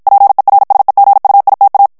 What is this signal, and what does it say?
Morse code, heard as a keyed tone, about 54 wpm, 800 Hz. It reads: PERSEGUITA
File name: ITALIEN texte clair 1 vitesse 50 mots minute